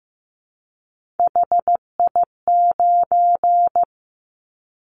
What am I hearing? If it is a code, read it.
Morse code, heard as a keyed tone, about 15 wpm, 700 Hz. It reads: HI9
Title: Callsign Prefixes for Top 100 DXCC - 15wpm